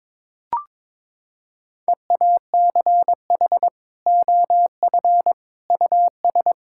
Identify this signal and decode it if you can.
Morse code, heard as a keyed tone, about 22 wpm, 700 Hz. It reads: EACH OF US
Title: Sentences from Top 1000 Words - Review Part 5 - 22wpm